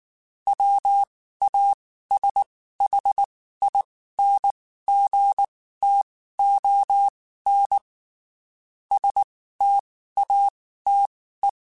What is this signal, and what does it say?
Morse code, heard as a keyed tone, about 19 wpm, spaced as for 14 wpm, 800 Hz. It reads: WASHINGTON STATE
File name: QSOs / QSOs-mp3s / WD0BGP 1@18